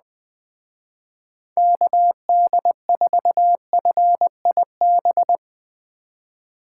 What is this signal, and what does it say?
Morse code, heard as a keyed tone, about 20 wpm, 700 Hz. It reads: KD4FIB
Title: Speed Racing - USA 2x3 Call Signs 17wpm